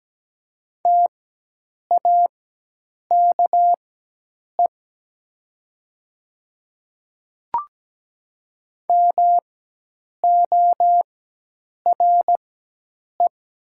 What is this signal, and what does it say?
Morse code, heard as a keyed tone, about 17 wpm, spaced as for 8 wpm, 700 Hz. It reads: TAKE MORE
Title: 4x Character Spacing - Top 200 Words - 17wpm